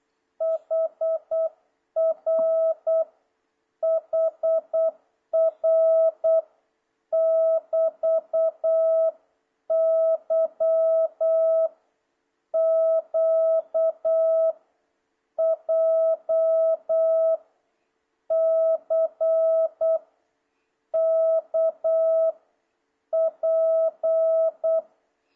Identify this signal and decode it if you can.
Morse code, heard as a keyed tone, about 8 wpm, 650 Hz. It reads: HR HR <BT> Y Q J C K P
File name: jh8cbh / 2026cw